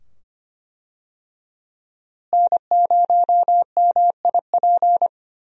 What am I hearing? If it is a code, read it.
Morse code, heard as a keyed tone, about 25 wpm, 700 Hz. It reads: N0MIP
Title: USA Call Signs - 5 Characters - Repeated 2x - 25wpm